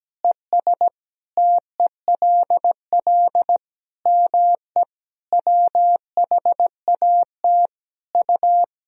Morse code, heard as a keyed tone, about 17 wpm, 700 Hz. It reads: ES TELL ME WHAT U
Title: Sentences from Top 200 Words and Abbreviations 17wpm